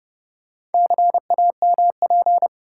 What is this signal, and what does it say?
Morse code, heard as a keyed tone, about 30 wpm, 700 Hz. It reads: CAMP
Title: Four Letter Words - 30wpm